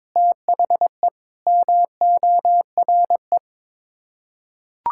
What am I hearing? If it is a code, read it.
Morse code, heard as a keyed tone, about 22 wpm, 700 Hz. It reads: THE MORE
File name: practice / 22wpm / Sets of 8 Words - 22wpm